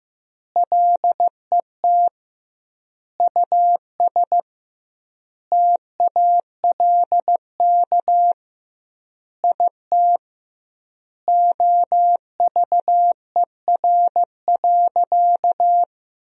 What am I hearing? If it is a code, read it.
Morse code, heard as a keyed tone, about 15 wpm, 700 Hz. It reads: LET US TALK IT OVER.
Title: 2x Word-Spacing - Sentences from Top 500 Words - 15wpm